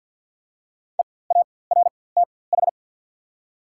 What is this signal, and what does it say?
Morse code, heard as a keyed tone, about 54 wpm, spaced as for 24 wpm, 690 Hz. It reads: EARTH
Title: 4x Character Spacing - Top 300 Words - 50wpm